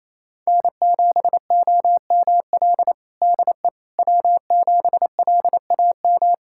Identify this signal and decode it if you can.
Morse code, heard as a keyed tone, about 27 wpm, 700 Hz. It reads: N7OML DE W7LAM
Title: USA Callsign DE Callsign - 5 Characters - 28wpm